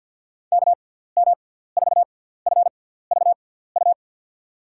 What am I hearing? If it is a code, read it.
Morse code, heard as a keyed tone, about 50 wpm, spaced as for 17 wpm, 690 Hz. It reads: KM4FVU
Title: USA Call Signs - 6x Character Spacing - 50wpm